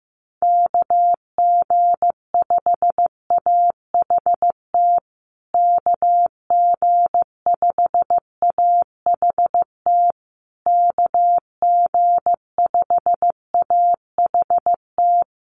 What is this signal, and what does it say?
Morse code, heard as a keyed tone, about 15 wpm, 700 Hz. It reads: KG5AHT KG5AHT KG5AHT
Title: USA Call Signs Repeated 3x 15wpm